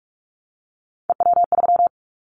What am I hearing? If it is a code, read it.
Morse code, heard as a keyed tone, about 46 wpm, 700 Hz. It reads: EW3
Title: Callsign Prefixes for Top 50 DXCC - 45wpm